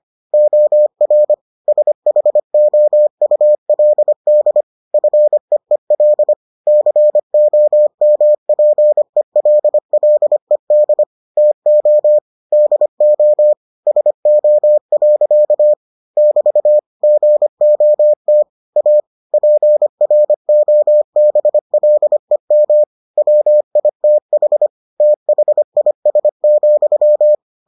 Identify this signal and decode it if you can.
Morse code, heard as a keyed tone, about 25 wpm, 600 Hz. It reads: OR SHOULD FEEL COMPELLED TO DO SO. <BT> GOT A PROBLEM WITH THIS,